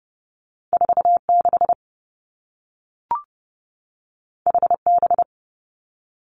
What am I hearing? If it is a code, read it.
Morse code, heard as a keyed tone, about 30 wpm, 700 Hz. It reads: 4 6 H B
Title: Two Challenging Letter-Number Combinations 30wpm